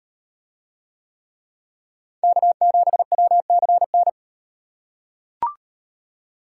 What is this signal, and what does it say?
Morse code, heard as a keyed tone, about 38 wpm, 700 Hz. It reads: K7WCN